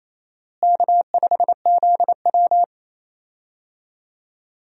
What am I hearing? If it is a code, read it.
Morse code, heard as a keyed tone, about 28 wpm, 700 Hz. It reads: K5ZW
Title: USA Call Signs - 4 Characters - Repeated 3x - 28wpm